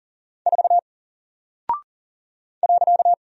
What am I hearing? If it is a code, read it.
Morse code, heard as a keyed tone, about 40 wpm, 700 Hz. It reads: 4 .
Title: Single Character (Letter:Number:Punctuation:Prosign) Alt 40wpm